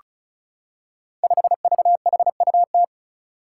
Morse code, heard as a keyed tone, about 36 wpm, 700 Hz. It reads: 5VHUT
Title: Five Challenging Letter-Number Combinations 35wpm